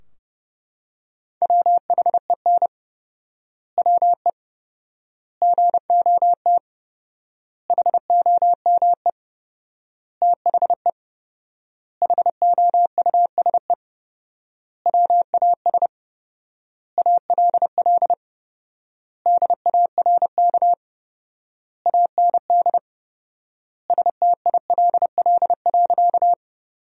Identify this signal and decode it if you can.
Morse code, heard as a keyed tone, about 28 wpm, 700 Hz. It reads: WHEN WE GOT HOME THE HOUSE WAS ALL DARK AND STILL.
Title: 4x Word-Spacing - Sentences from Top 500 Words - 30wpm